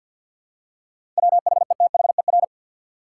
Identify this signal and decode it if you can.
Morse code, heard as a keyed tone, about 51 wpm, 700 Hz. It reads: WHETHER
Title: Seven Letter Words - 50wpm